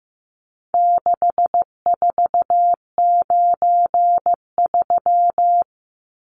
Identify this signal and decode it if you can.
Morse code, heard as a keyed tone, about 15 wpm, 700 Hz. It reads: 6493
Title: Numbers - 4-Digit 15wpm